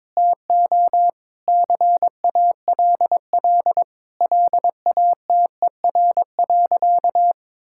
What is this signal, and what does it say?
Morse code, heard as a keyed tone, about 22 wpm, 700 Hz. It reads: TO CALL LATER.
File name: practice / 22wpm / Sentences with 4-Word Phrases - 22wpm